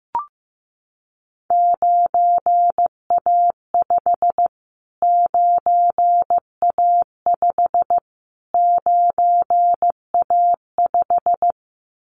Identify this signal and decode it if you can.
Morse code, heard as a keyed tone, about 15 wpm, 700 Hz. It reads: 9A5 9A5 9A5
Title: Callsign Prefixes for Top 25 DXCC - Repeated 3x - 15wpm